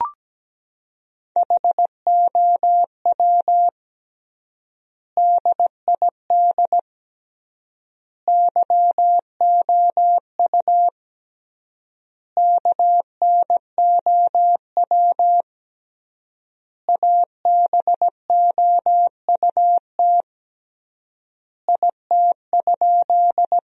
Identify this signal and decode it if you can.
Morse code, heard as a keyed tone, about 17 wpm, 700 Hz. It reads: HOW DID YOU KNOW ABOUT IT?
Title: 3x Word-Spacing - Sentences from Top 100 Words - 17wpm